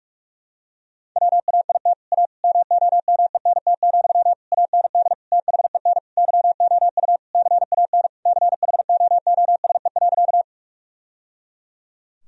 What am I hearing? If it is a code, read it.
Morse code, heard as a keyed tone, about 46 wpm, 700 Hz. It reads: WAIT A MOMENT, AND THEN YOU CAN CHOOSE.
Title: Sentences from Top 1000 Words - Review Part 4 - 45wpm